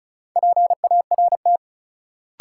Morse code, heard as a keyed tone, about 36 wpm, 700 Hz. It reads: PART
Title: Sentences from Top 200 Words - New Word Drill - 35wpm